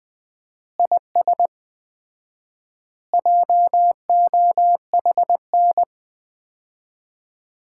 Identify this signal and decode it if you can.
Morse code, heard as a keyed tone, about 20 wpm, 700 Hz. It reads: IS JOHN
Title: Common QSO Phrases - 4x Word-Spacing - 20wpm